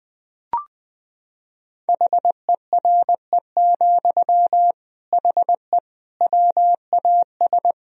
Morse code, heard as a keyed tone, about 21 wpm, 700 Hz. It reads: HERE, HE WAS
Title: Sentences from Top 1000 Words - Review Part 3 - 20wpm